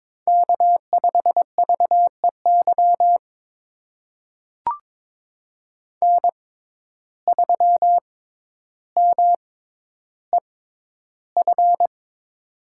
Morse code, heard as a keyed tone, about 22 wpm, spaced as for 11 wpm, 700 Hz. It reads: K5VEY N3MEF
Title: USA Call Signs - 5 Characters - 6x Character Spacing - 22wpm